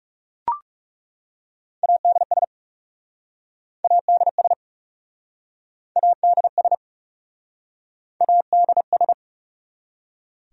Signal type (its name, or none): Morse code